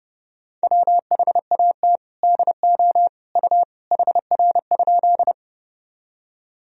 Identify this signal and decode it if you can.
Morse code, heard as a keyed tone, about 30 wpm, 700 Hz. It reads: WHAT DO U HR?
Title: Sentences from Top 300 Words and Abbreviations 30wpm